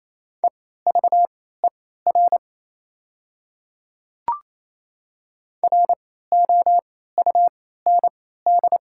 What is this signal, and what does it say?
Morse code, heard as a keyed tone, about 27 wpm, spaced as for 16 wpm, 700 Hz. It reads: EVER ROUND
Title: 3x Character Spacing - Top 500 Words - 28wpm